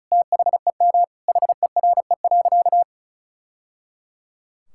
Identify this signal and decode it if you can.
Morse code, heard as a keyed tone, about 36 wpm, 700 Hz. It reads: THEM HERE.
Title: Sentences from Top 1000 Words - Review Part 5 - 35wpm